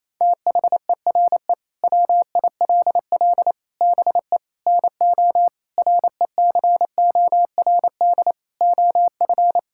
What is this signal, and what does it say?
Morse code, heard as a keyed tone, about 28 wpm, 700 Hz. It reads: THERE WILL BE NO RECORD OF
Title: Sentences from Top 1000 Words - Review Part 5 - 28wpm